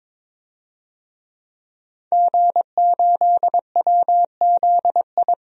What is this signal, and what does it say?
Morse code, heard as a keyed tone, about 22 wpm, 700 Hz. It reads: G8WZI
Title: DXCC-100 Call Signs - 5 Characters - 22wpm